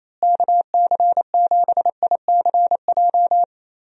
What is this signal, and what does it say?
Morse code, heard as a keyed tone, about 28 wpm, 700 Hz. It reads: KC7ICJ
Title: USA Call Signs - 6 Characters - Repeated 3x - 28wpm